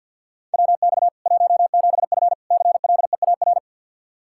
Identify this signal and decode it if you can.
Morse code, heard as a keyed tone, about 52 wpm, 700 Hz. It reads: WX 17F CLEAR
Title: Practice QSO - Exchange 2 - Info - 50wpm